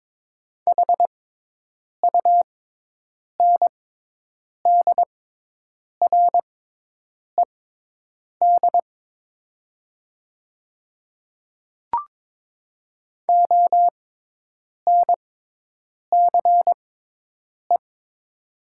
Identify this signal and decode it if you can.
Morse code, heard as a keyed tone, about 22 wpm, spaced as for 8 wpm, 700 Hz. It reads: HUNDRED ONCE
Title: 6x Character Spacing - Top 500 Words - 22wpm